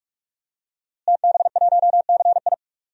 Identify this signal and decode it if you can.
Morse code, heard as a keyed tone, about 46 wpm, 700 Hz. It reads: TB1CI
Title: DXCC-100 Call Signs - 5 Characters - Repeated 3x - 45wpm